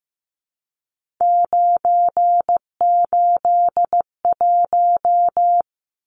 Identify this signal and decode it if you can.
Morse code, heard as a keyed tone, about 15 wpm, 700 Hz. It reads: 981
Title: Numbers - 3-Digit 15wpm